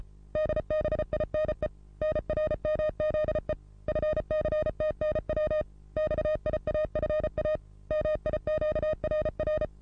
Morse code, heard as a keyed tone, about 32 wpm, 620 Hz. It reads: DBINE NRMZE FCTNW XIAFA MIQRR
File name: repos / Morsekurs / DJ4UF Morsen27